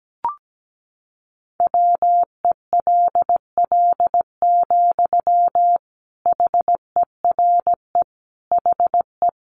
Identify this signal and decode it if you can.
Morse code, heard as a keyed tone, about 17 wpm, 700 Hz. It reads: WELL, HERE HE
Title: Sentences from Top 200 Words 17wpm